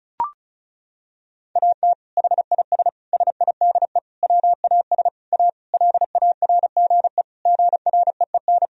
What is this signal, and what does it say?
Morse code, heard as a keyed tone, about 36 wpm, 700 Hz. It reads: AT HIS SIDE WAS A LARGE GREEN
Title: Sentences from Top 1000 Words - Review Part 2 - 35wpm